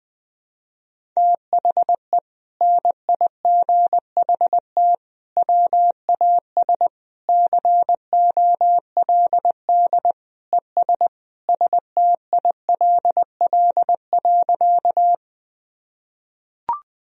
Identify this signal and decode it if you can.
Morse code, heard as a keyed tone, about 20 wpm, 700 Hz. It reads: THE NIGHT WAS COLD ES STILL.
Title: Sentences from Top 400 Words and Abbreviations 20wpm